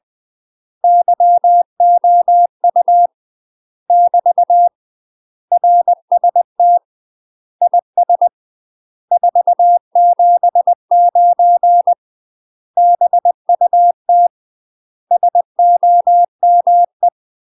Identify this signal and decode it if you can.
Morse code, heard as a keyed tone, about 20 wpm, 700 Hz. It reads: YOU <BT> RST IS 479 BUT SOME
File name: UK MASTER QSOs TRAINING / 20WPM / UK MASTER QSOs 0040